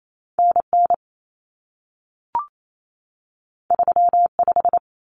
Morse code, heard as a keyed tone, about 28 wpm, 700 Hz. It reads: NN 35
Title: Common Cut Numbers - 2-Digit 28wpm